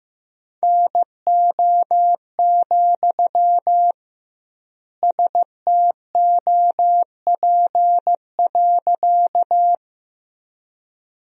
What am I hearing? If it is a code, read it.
Morse code, heard as a keyed tone, about 15 wpm, 700 Hz. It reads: NO, STOP.
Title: 2x Word-Spacing - Sentences from Top 300 Words - 15wpm